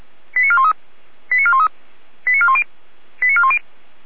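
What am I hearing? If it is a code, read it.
Five-tone selective call (ZVEI), heard as quick runs of five tones, tones of about 70 ms. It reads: 89413 89413 89410 89410
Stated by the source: ZVEI five-tone selective call received on an SDR.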